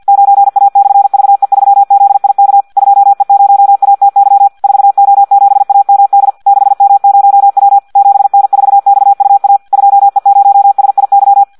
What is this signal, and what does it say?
Morse code, heard as a keyed tone, about 50 wpm, 800 Hz. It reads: 9A/VE3ZIK 2E0ATY 4O7AMD 6M0W 7N4XUA 2E0SIY
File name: Audio practica CW / qrq-tx-50 2